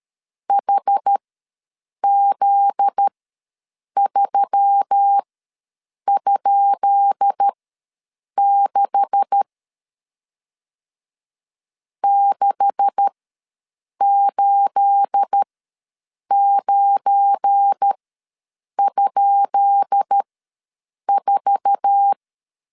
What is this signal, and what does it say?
Morse code, heard as a keyed tone, about 13 wpm, spaced as for 7 wpm, 800 Hz. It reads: HZ3?6 689?4
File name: LEZIONI / 10 SIGNS / 10B INT ws 12wpm 3527